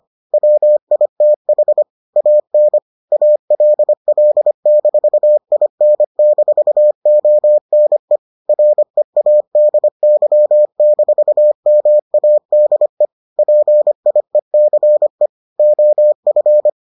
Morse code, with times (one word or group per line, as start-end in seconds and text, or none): 0.3-1.8 WITH
2.2-2.8 AN
3.1-8.2 ALL-IN-ONE
8.5-13.1 READY-MADE
13.4-15.3 PIECE
15.6-16.7 OF